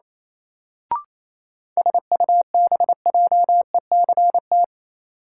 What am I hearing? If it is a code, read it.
Morse code, heard as a keyed tone, about 28 wpm, 700 Hz. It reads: SUBJECT
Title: Sentences from Top 800 Words - New Word Drill - 28wpm